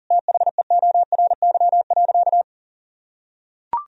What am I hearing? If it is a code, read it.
Morse code, heard as a keyed tone, about 37 wpm, 700 Hz. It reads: THEORY.
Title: Sentences from Top 1000 Words - Review Part 3 - 40wpm